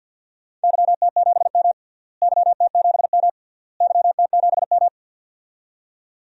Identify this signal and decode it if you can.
Morse code, heard as a keyed tone, about 53 wpm, 700 Hz. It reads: YT7M YT7M YT7M
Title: DXCC-100 Call Signs - 4 Characters - Repeated 3x - 50wpm